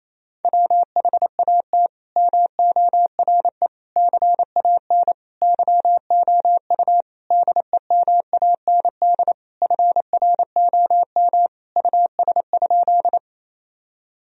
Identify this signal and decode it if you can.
Morse code, heard as a keyed tone, about 28 wpm, 700 Hz. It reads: WHAT MORE CAN YOU DEMAND FROM US?
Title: Sentences from Top 1000 Words - Review Part 4 - 28wpm